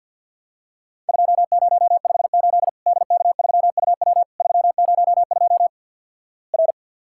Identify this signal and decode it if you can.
Morse code, heard as a keyed tone, about 50 wpm, 700 Hz. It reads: 2058 DK3UW 302 R
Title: Worked All Europe DX Contest - QTC Exchange with Repeats - 50wpm